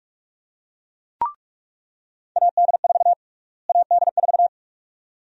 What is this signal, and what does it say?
Morse code, heard as a keyed tone, about 47 wpm, 700 Hz. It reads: AD4 AD4
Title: USA Call Sign Prefixes - Repeated 2x - 45wpm